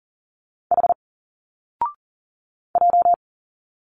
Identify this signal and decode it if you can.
Morse code, heard as a keyed tone, about 41 wpm, 700 Hz. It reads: H J
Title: Single Letters 40wpm